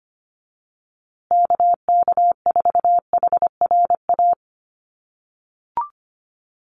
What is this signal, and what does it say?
Morse code, heard as a keyed tone, about 25 wpm, 700 Hz. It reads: KK4HRA
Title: USA Call Signs Repeated 2x 25wpm